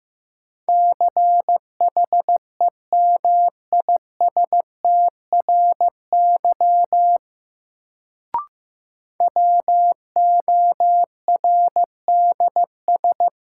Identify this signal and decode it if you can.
Morse code, heard as a keyed tone, about 15 wpm, 700 Hz. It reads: CHEMISTRY WORDS
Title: Sentences from Top 1000 Words - Review Part 6 - New Word Drill - 15wpm